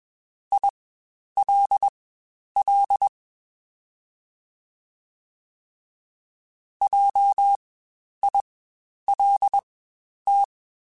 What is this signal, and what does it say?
Morse code, heard as a keyed tone, about 21 wpm, spaced as for 10 wpm, 800 Hz. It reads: ILL JILT
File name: Spaced Alphas / Spaced Alphas mp3s / Js sp @20